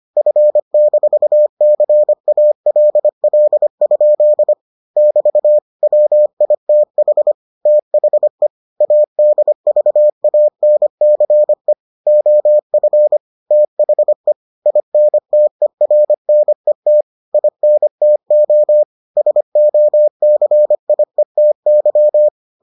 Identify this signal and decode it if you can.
Morse code, heard as a keyed tone, about 25 wpm, 600 Hz. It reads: F-CALL? <BT> WITH THE ADVANCE OF THE INTERNET INTO SOCIETY